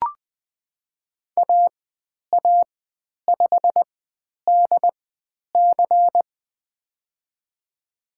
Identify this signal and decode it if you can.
Morse code, heard as a keyed tone, about 20 wpm, spaced as for 10 wpm, 700 Hz. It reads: AA5DC
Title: 095 Lesson 16g - Letter c - Call Signs 20wpm